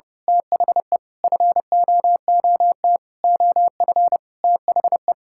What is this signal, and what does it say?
Morse code, heard as a keyed tone, about 32 wpm, 700 Hz. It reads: THE FOOT OF THE
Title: Sentences from Top 1000 Words - Review Part 5 - 30wpm